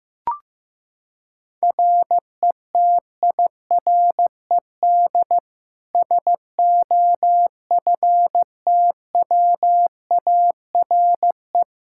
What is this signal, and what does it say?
Morse code, heard as a keyed tone, about 15 wpm, 700 Hz. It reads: RETIRED SOFTWARE
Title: Practice QSO - Exchange 3 - More Info - 15wpm